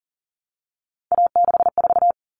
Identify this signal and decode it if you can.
Morse code, heard as a keyed tone, about 40 wpm, 700 Hz. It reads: A64
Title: Common Cut Numbers - 3-Digit 40wpm